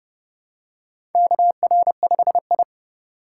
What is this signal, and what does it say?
Morse code, heard as a keyed tone, about 30 wpm, 700 Hz. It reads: KR5I
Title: USA Call Signs - 4 Characters - 2x Character Spacing - 30wpm